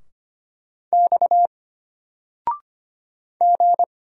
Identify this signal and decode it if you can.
Morse code, heard as a keyed tone, about 25 wpm, 700 Hz. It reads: X G
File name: Single-Letter-Number-25wpm